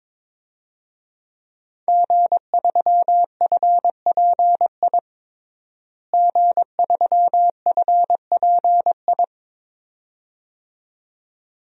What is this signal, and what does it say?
Morse code, heard as a keyed tone, about 22 wpm, 700 Hz. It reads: G3FPI G3FPI
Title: DXCC-100 Call Signs - 5 Characters - Repeated 2x - 22wpm